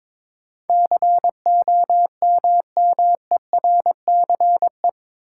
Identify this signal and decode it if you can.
Morse code, heard as a keyed tone, about 22 wpm, 700 Hz. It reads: COMMERCE
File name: practice / 22wpm / Eight Letter Words - 22wpm